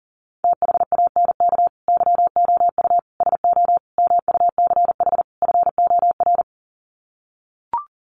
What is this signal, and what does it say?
Morse code, heard as a keyed tone, about 40 wpm, 700 Hz. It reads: THANK YOU SO MUCH FOR
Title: Sets of 5 Words 40wpm